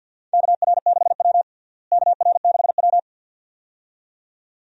Morse code, heard as a keyed tone, about 52 wpm, 700 Hz. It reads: KR6W KR6W
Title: USA Call Signs - 4 Characters - Repeated 2x - 50wpm